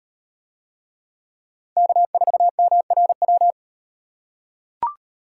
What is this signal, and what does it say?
Morse code, heard as a keyed tone, about 38 wpm, 700 Hz. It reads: K4MRW